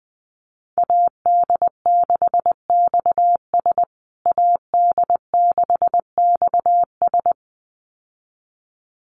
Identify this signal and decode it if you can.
Morse code, heard as a keyed tone, about 20 wpm, 700 Hz. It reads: AD6XS AD6XS
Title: USA Call Signs Repeated 2x 20wpm